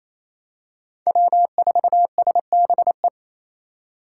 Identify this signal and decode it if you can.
Morse code, heard as a keyed tone, about 28 wpm, 700 Hz. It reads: W4SBE
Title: USA Call Signs - 5 Characters - 2x Character Spacing - 28wpm